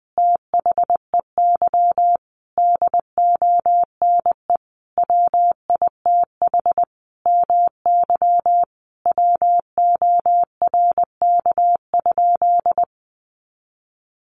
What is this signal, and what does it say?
Morse code, heard as a keyed tone, about 19 wpm, 700 Hz. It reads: THEY DONE WITH MY WORK?
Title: Sentences from Top 500 Words 20wpm